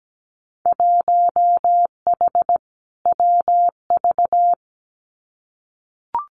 Morse code, heard as a keyed tone, about 17 wpm, 700 Hz. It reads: 1H WV
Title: Winter Field Day Exchanges 17wpm